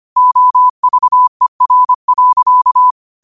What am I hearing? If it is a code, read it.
Morse code, heard as a keyed tone, about 25 wpm, 1000 Hz. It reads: OVER.
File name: qotdfiles / 25wpm